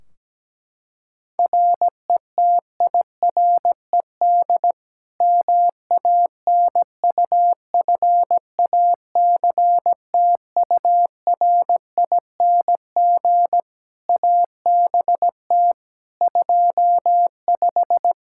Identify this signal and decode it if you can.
Morse code, heard as a keyed tone, about 17 wpm, 700 Hz. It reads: RETIRED MANUFACTURING ABT 25
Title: K1USN SST QSO Practice 17wpm